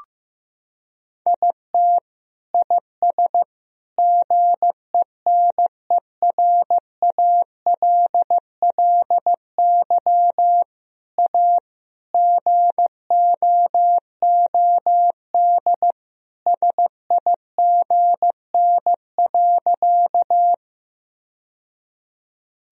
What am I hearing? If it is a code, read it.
Morse code, heard as a keyed tone, about 15 wpm, 700 Hz. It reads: IT IS GENERALLY A GOOD SIGN.
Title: Sentences from Top 1000 Words - Review Part 4 - 15wpm